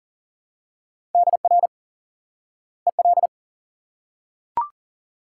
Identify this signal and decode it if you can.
Morse code, heard as a keyed tone, about 37 wpm, 700 Hz. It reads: DR EL
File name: IQE Course 3 - Lesson 89 - YAGI - Two Elements - 40wpm